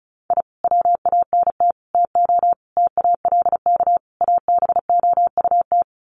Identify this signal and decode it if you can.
Morse code, heard as a keyed tone, about 34 wpm, 700 Hz. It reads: I WANT TO TALK ABOUT
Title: Sets of 5 Words Encore 35wpm